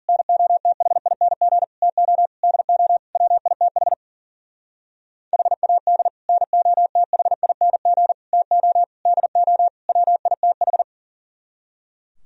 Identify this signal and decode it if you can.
Morse code, heard as a keyed tone, about 52 wpm, 700 Hz. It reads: NOTHING TO DO WITH HAD NOTHING TO DO WITH